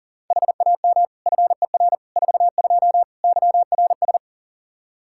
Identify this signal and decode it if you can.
Morse code, heard as a keyed tone, about 40 wpm, 700 Hz. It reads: HAM FER 42 YRS
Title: Practice QSO - Exchange 3 - More Info - 40wpm